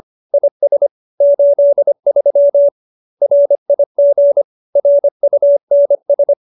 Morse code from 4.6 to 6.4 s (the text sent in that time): RUNS